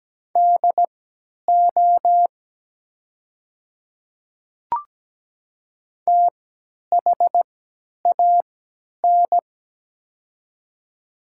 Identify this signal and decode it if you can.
Morse code, heard as a keyed tone, about 17 wpm, spaced as for 10 wpm, 700 Hz. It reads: DO THAN
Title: 3x Character Spacing - Top 100 Words - 17wpm